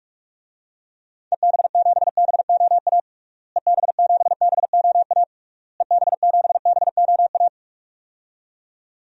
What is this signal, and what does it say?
Morse code, heard as a keyed tone, about 41 wpm, 700 Hz. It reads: EB7BOA EB7BOA EB7BOA
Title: DXCC-100 Call Signs - 6 Characters - Repeated 3x - 45wpm